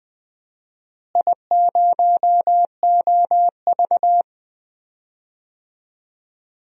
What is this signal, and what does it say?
Morse code, heard as a keyed tone, about 21 wpm, 700 Hz. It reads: I0OV
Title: DXCC-100 Call Signs - 4 Characters - Repeated 2x - 20wpm